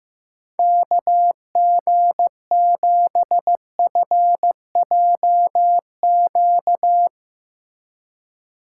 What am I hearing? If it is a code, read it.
Morse code, heard as a keyed tone, about 16 wpm, 700 Hz. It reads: KG7FJQ
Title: Speed Racing - USA Call Signs 15wpm